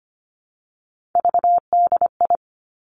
Morse code, heard as a keyed tone, about 25 wpm, 700 Hz. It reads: VDI
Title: Three Challenging Letter-Number Combinations 25wpm